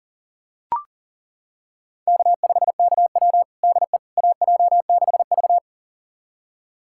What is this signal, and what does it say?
Morse code, heard as a keyed tone, about 40 wpm, 700 Hz. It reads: K5KW DE AJ6V
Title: USA Callsign DE Callsign - 4 Characters - 40wpm